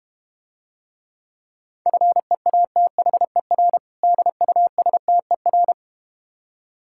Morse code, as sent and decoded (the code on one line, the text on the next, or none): ..-. .- - .... .-. / -.. ..- ... - . .-.
FEATHER DUSTER